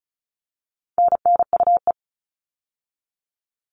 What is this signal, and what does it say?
Morse code, heard as a keyed tone, about 34 wpm, 700 Hz. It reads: NNUE
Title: All Cut Numbers - 4-Digit 35wpm